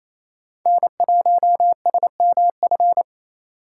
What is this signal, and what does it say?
Morse code, heard as a keyed tone, about 28 wpm, 700 Hz. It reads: N1SMF
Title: USA Call Signs - 5 Characters - 4x Character Spacing - 28wpm